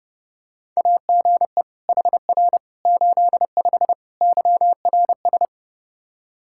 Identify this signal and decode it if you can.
Morse code, heard as a keyed tone, about 30 wpm, 700 Hz. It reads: AGE HR 85 YRS
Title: Practice QSO - Exchange 3 - More Info - 30wpm